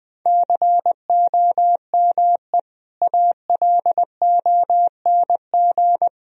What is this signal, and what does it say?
Morse code, heard as a keyed tone, about 20 wpm, 700 Hz. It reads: COME ALONG
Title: Sentences from Top 1000 Words - Review Part 4 - 20wpm